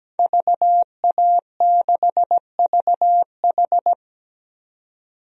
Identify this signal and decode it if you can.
Morse code, heard as a keyed tone, about 17 wpm, 700 Hz. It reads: VA6VH
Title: DXCC-100 Call Signs - 5 Characters - Repeated 3x - 17wpm